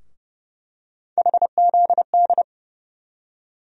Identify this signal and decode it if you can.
Morse code, heard as a keyed tone, about 30 wpm, 700 Hz. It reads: HZD
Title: USA Call Sign Postfixes - 3 Letters 30wpm